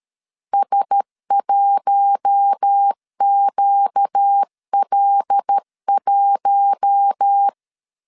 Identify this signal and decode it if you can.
Morse code, heard as a keyed tone, about 13 wpm, 800 Hz. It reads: S1QL1
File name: CW / DET / 1 6 1500C S B 12wpm 3431